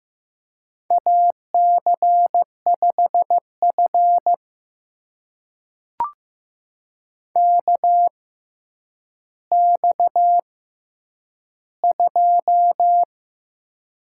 Morse code, heard as a keyed tone, about 15 wpm, spaced as for 9 wpm, 700 Hz. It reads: AC5F KX2
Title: USA Call Signs - 4 Characters - 6x Character Spacing - 15wpm